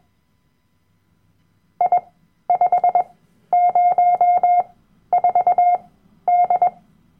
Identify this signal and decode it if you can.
Morse code, heard as a keyed tone, about 21 wpm, spaced as for 12 wpm, 710 Hz. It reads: I504D
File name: Workshop Dateien / CW-Online-Kurs / Lektion 010